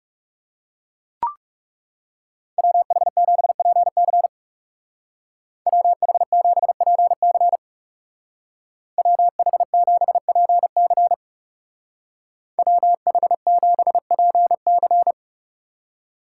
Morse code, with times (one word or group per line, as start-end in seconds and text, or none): 2.6-4.3 WH7PC
5.7-7.6 WH7PC
9.0-11.1 WH7PC
12.6-15.1 WH7PC